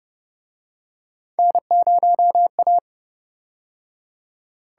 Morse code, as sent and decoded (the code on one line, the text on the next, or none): -. ----- .-
N0A